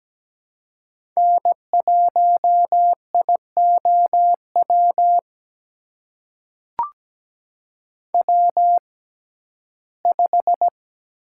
Morse code, heard as a keyed tone, about 17 wpm, spaced as for 12 wpm, 700 Hz. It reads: N1IOW W5
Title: USA Call Signs - 6x Character Spacing - 17wpm